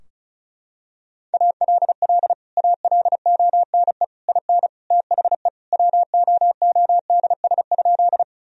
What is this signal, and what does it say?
Morse code, heard as a keyed tone, about 34 wpm, 700 Hz. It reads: ALL ALONE IN THE WOODS?
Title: Sentences from Top 1000 Words - Review Part 2 - 35wpm